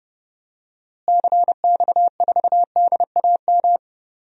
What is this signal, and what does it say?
Morse code, heard as a keyed tone, about 30 wpm, 700 Hz. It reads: CX4DAM
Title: DXCC-100 Call Signs - 6 Characters - Repeated 2x - 30wpm